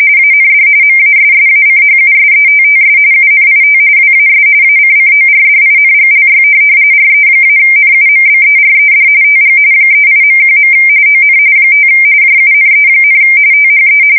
RTTY, about 45.45 baud. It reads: AND MASSACHUSETTS UNIT WAS ACCORDINGLY RE-NUMBERED TG 38.1. ON 5 JUNE, THE
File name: fsk170 / USS-Massachusetts